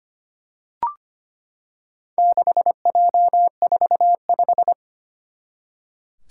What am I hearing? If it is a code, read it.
Morse code, heard as a keyed tone, about 25 wpm, 700 Hz. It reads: 6J45
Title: Four Challenging Letter-Number Combinations 25wpm